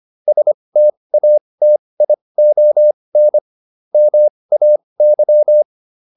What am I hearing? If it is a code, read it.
Morse code, heard as a keyed tone, about 25 wpm, spaced as for 20 wpm, 600 Hz. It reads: STATION MAY